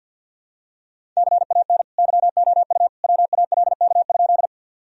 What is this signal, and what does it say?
Morse code, heard as a keyed tone, about 52 wpm, 700 Hz. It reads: CAN YOU WALK?
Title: Sentences from Top 1000 Words - Review Part 1 - 50wpm